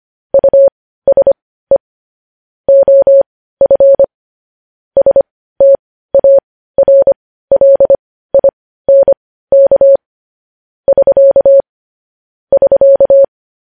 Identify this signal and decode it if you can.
Morse code, heard as a keyed tone, about 25 wpm, spaced as for 15 wpm, 550 Hz. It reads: USE OF STARLINK <SK> <SK>
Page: News Headlines in Morse Code at 15 WPM